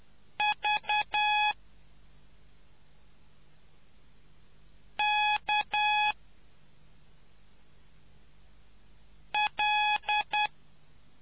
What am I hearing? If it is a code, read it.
Morse code, heard as a keyed tone, about 10 wpm, 840 Hz. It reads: V K L